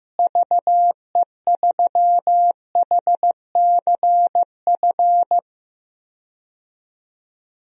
Morse code, heard as a keyed tone, about 15 wpm, 700 Hz. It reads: VE3HCF
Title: DXCC-100 Call Signs - 6 Characters - Repeated 3x - 15wpm